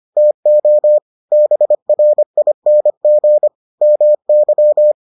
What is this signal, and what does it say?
Morse code, heard as a keyed tone, about 25 wpm, 600 Hz. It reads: TO BRING MY